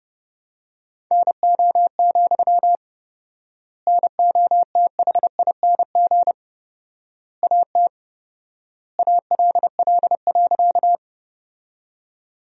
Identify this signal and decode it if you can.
Morse code, heard as a keyed tone, about 30 wpm, 700 Hz. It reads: NO, NOTHING AT ALL.
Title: 4x Word-Spacing - Sentences from Top 500 Words - 30wpm